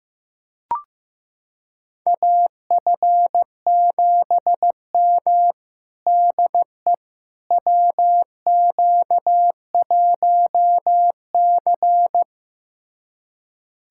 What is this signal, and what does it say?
Morse code, heard as a keyed tone, about 16 wpm, 700 Hz. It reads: AF7M DE WQ1C
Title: USA Callsign DE Callsign - 4 Characters - 15wpm